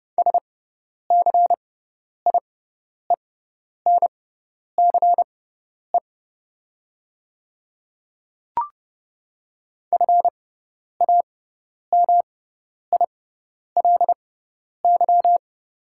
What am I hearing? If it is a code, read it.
Morse code, heard as a keyed tone, about 30 wpm, spaced as for 10 wpm, 700 Hz. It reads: SCIENCE FAMILY
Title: 6x Character Spacing - Top 500 Words - 30wpm